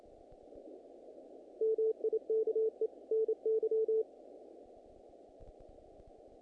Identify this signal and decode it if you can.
Morse code, heard as a keyed tone, about 30 wpm, 440 Hz. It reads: MIKE NY
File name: contests / naqp-cw-k5tr-op-k5pi- -2000